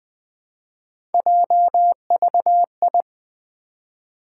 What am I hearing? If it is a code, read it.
Morse code, heard as a keyed tone, about 20 wpm, 700 Hz. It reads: JVI
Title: Three Letter-Number Combinations - 20wpm